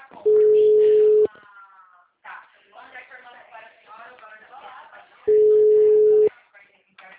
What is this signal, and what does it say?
Ringback tone (Europe, 425 Hz), with speech.
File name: out- -2023- - - .32910